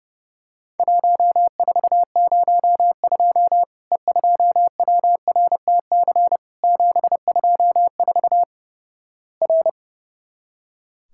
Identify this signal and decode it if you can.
Morse code, heard as a keyed tone, about 32 wpm, 700 Hz. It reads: 1402 E2WRTC 724 R
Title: Worked All Europe DX Contest - QTC Exchange - 30wpm